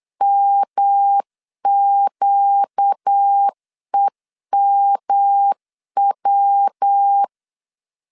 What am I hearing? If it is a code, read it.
Morse code, heard as a keyed tone, about 8 wpm, 800 Hz. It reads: MQEMW
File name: CW / DET / 1 6 1500C S C 8wpm 5219